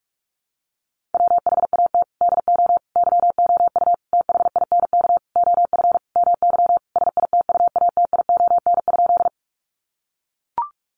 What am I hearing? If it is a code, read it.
Morse code, heard as a keyed tone, about 44 wpm, 700 Hz. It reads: WHAT DO YOU THINK OF MY SITUATION?
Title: Sentences from Top 1200 Words 45wpm